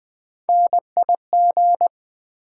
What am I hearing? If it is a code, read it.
Morse code, heard as a keyed tone, about 20 wpm, 700 Hz. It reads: NIG
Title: Build A Word - 3 Letters At A Time - 9-Letter Words - 20wpm